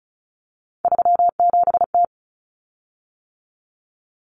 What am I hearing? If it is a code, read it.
Morse code, heard as a keyed tone, about 34 wpm, 700 Hz. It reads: 37T
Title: Common Cut Numbers - 3-Digit 35wpm